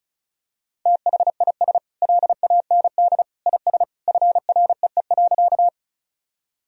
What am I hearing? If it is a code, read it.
Morse code, heard as a keyed tone, about 36 wpm, 700 Hz. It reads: THIS LAND IS FREE.
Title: Sentences from Top 1000 Words - Review Part 1 - 35wpm